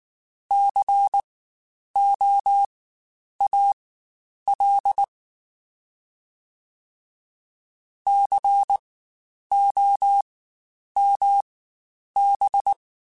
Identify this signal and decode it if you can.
Morse code, heard as a keyed tone, about 19 wpm, spaced as for 9 wpm, 800 Hz. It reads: COAL COMB
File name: Spaced Alphas / Spaced Alphas mp3s / Cs sp @18